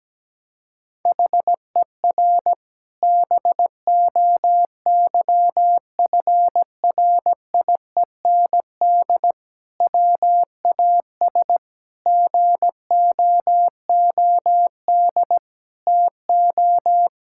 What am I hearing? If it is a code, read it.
Morse code, heard as a keyed tone, about 17 wpm, 700 Hz. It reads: HER BOYFRIEND WAS GOOD TO
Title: Sentences from Top 1000 Words - Review Part 5 - 17wpm